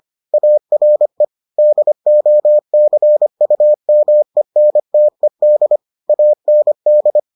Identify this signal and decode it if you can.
Morse code, heard as a keyed tone, about 25 wpm, 600 Hz. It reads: ARE DOCUMENTED AND